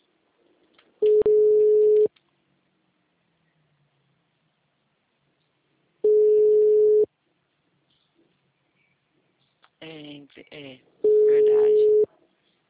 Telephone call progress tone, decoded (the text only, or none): ringback tone